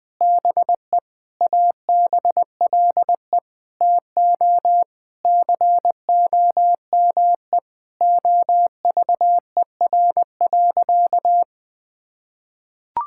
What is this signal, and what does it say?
Morse code, heard as a keyed tone, about 19 wpm, 700 Hz. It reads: BE ABLE TO COME OVER.
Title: Sentences with 4-Word Phrases - 20wpm